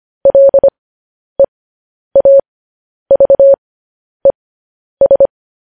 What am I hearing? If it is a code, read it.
Morse code, heard as a keyed tone, about 25 wpm, spaced as for 10 wpm, 550 Hz. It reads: LEAVES
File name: podcast / News headlines-1x-25x10